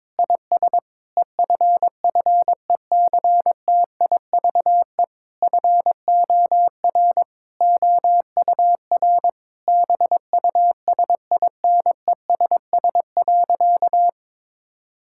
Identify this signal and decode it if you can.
Morse code, heard as a keyed tone, about 22 wpm, 700 Hz. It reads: IS EFFECTIVE FOR OUR BUSINESS.
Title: Sentences from Top 1000 Words - Review Part 6 - 22wpm